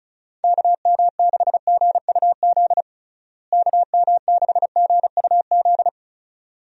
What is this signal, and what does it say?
Morse code, heard as a keyed tone, about 36 wpm, 700 Hz. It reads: KM6GUZ KM6GUZ
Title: USA Call Signs - 6 Characters - Repeated 2x - 35wpm